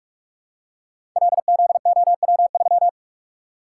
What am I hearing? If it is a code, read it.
Morse code, heard as a keyed tone, about 44 wpm, 700 Hz. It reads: LZOW3
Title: Code Groups - 5 Letter:Numbers - 45wpm